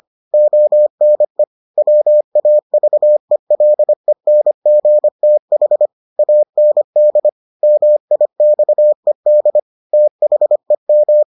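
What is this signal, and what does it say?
Morse code, heard as a keyed tone, about 25 wpm, 600 Hz. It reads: ONE WAVELENGTH AND MIXED THEM
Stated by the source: Morse edition of the Foundations of Amateur Radio episodes.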